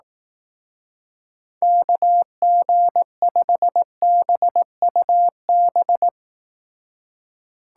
Morse code, heard as a keyed tone, about 18 wpm, 700 Hz. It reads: KG5BUB
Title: Speed Racing - USA Call Signs 15wpm